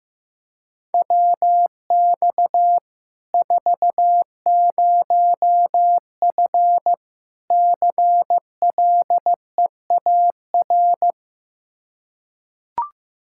Morse code, heard as a keyed tone, about 15 wpm, 700 Hz. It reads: WX 40F CLEAR
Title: Practice QSO - Exchange 2 - Info - 15wpm